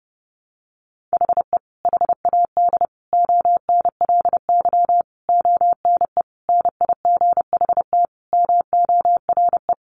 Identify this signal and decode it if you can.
Morse code, heard as a keyed tone, about 30 wpm, 700 Hz. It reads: HE HAD ONLY ONE NIGHT MORE
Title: Sentences from Top 300 Words 30wpm